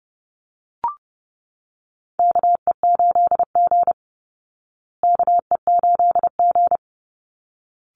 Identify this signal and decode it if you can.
Morse code, heard as a keyed tone, about 30 wpm, 700 Hz. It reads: KE8G KE8G
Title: Call Signs Repeated 3x - 30wpm